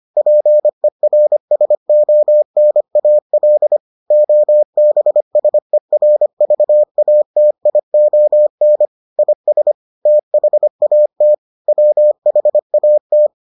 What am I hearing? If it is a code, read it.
Morse code, heard as a keyed tone, about 25 wpm, 600 Hz. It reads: PERSONAL OBSERVATION IS THAT WHAT